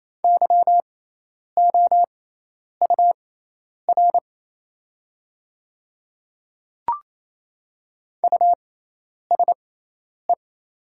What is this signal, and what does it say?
Morse code, heard as a keyed tone, about 28 wpm, spaced as for 10 wpm, 690 Hz. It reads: YOUR USE
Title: 6x Character Spacing - Top 100 Words - 28wpm